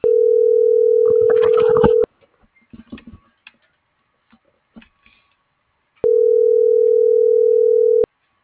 Ringback tone (North America), 440 + 480 Hz.